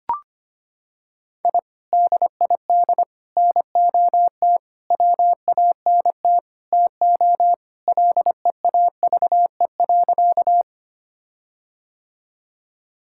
Morse code, heard as a keyed tone, about 25 wpm, 700 Hz. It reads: I DID NOT WANT TO LEAVE.
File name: practice / 25wpm / Sentences with 4-Word Phrases - 25wpm